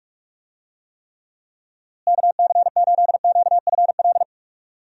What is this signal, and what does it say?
Morse code, heard as a keyed tone, about 46 wpm, 700 Hz. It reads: KC8QFL
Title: USA Call Signs - 6 Characters - Repeated 3x - 45wpm